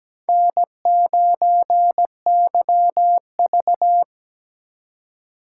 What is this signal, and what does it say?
Morse code, heard as a keyed tone, about 17 wpm, 700 Hz. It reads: N9YV
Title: USA Call Signs - 3x Character Spacing - 17wpm